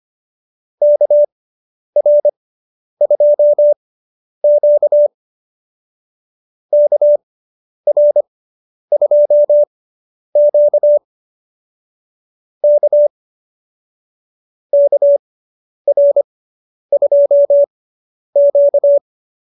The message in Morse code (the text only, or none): KR2Q KR2Q K KR2Q